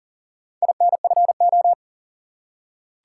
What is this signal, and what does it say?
Morse code, heard as a keyed tone, about 38 wpm, 700 Hz. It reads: INFO
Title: Top 180 QSO Elements 40wpm